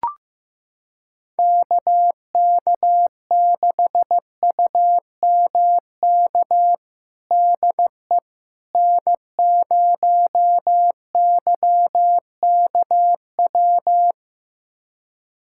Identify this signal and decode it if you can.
Morse code, heard as a keyed tone, about 15 wpm, 700 Hz. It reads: KK6UMK DE N0YKW
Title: USA Callsign DE Callsign 15wpm